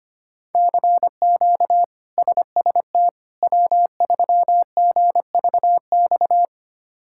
Morse code, heard as a keyed tone, about 25 wpm, 700 Hz. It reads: CQ SST W3GVX
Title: K1USN SST QSO Practice - 25wpm